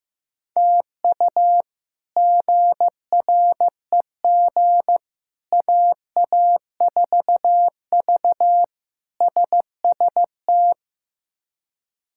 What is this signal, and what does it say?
Morse code, heard as a keyed tone, about 15 wpm, 700 Hz. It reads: TU GREG AA4V SST
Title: K1USN SST QSO Practice - 15wpm